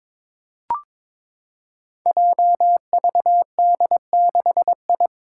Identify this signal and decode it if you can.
Morse code, heard as a keyed tone, about 22 wpm, 700 Hz. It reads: JVD6I
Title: Five Challenging Letter-Number Combinations 22wpm